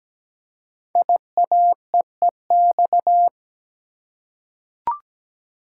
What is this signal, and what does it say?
Morse code, heard as a keyed tone, about 17 wpm, 700 Hz. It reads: IAEEX
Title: Code Groups - 5 Letter:Numbers - 17wpm